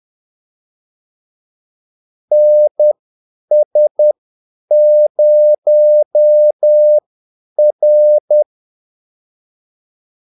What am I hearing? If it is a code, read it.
Morse code, heard as a keyed tone, about 10 wpm, spaced as for 8 wpm, 600 Hz. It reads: NS0R
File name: Callsigns-10-8wpm